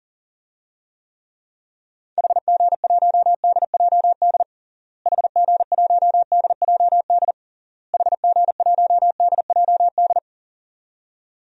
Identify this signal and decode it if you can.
Morse code, heard as a keyed tone, about 40 wpm, 700 Hz. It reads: HG1DJD HG1DJD HG1DJD
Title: DXCC-100 Call Signs - 6 Characters - Repeated 3x - 40wpm